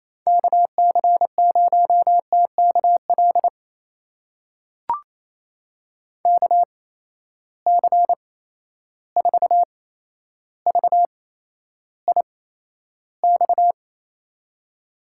Morse code, heard as a keyed tone, about 28 wpm, spaced as for 12 wpm, 700 Hz. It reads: KC0TKL KC4VIX
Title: USA Call Signs - 6 Characters - 8x Character Spacing - 28wpm